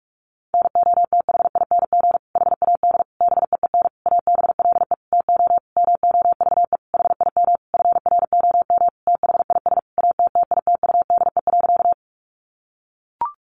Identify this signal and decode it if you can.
Morse code, heard as a keyed tone, about 44 wpm, 700 Hz. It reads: NOTHING HAD BEEN ABLE TO MOVE HIM FROM THIS ATTITUDE.
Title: Sentences from Top 1300 Words 45wpm